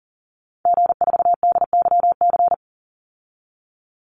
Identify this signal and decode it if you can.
Morse code, heard as a keyed tone, about 41 wpm, 700 Hz. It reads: G4DYC
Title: Call Signs - 40wpm